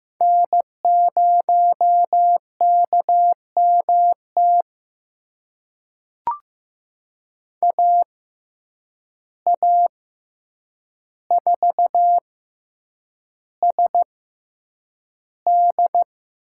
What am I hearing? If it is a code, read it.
Morse code, heard as a keyed tone, about 15 wpm, spaced as for 8 wpm, 700 Hz. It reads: N0KMT AA4SD
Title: USA Call Signs - 5 Characters - 6x Character Spacing - 15wpm